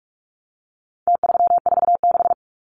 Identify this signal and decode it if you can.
Morse code, heard as a keyed tone, about 44 wpm, 700 Hz. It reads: T346
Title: Numbers - Contest Style - 4-Digit 45wpm